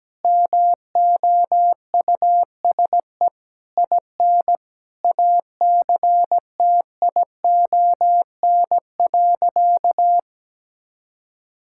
Morse code, heard as a keyed tone, about 17 wpm, 700 Hz. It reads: MOUSE IN ACTION.
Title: Sentences from Top 1000 Words - Review Part 5 - 17wpm